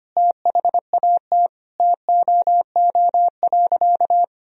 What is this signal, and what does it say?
Morse code, heard as a keyed tone, about 25 wpm, 700 Hz. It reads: THAT TOO.
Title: Sentences from Top 1000 Words - Review Part 3 - 25wpm